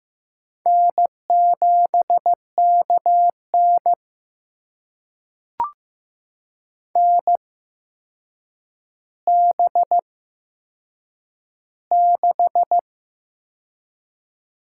Morse code, heard as a keyed tone, about 15 wpm, spaced as for 7 wpm, 700 Hz. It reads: N7KN NB6
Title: USA Call Signs - 4 Characters - 8x Character Spacing - 15wpm